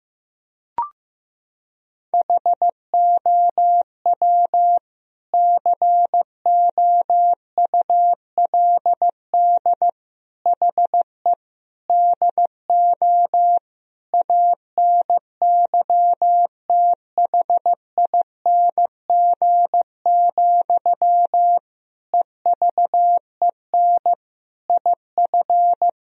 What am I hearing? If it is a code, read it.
Morse code, heard as a keyed tone, about 15 wpm, 700 Hz. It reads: HOW COULD HE DO ANYTHING, EVEN IF
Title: Sentences from Top 1000 Words - Review Part 1 - 15wpm